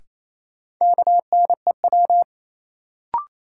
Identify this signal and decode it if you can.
Morse code, heard as a keyed tone, about 28 wpm, 700 Hz. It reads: KNEW
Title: Sentences from Top 400 Words - New Word Drill - 28wpm